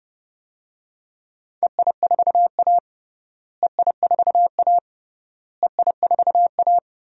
Morse code, heard as a keyed tone, about 30 wpm, 700 Hz. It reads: EI4A EI4A EI4A
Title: DXCC-100 Call Signs - 4 Characters - Repeated 3x - 30wpm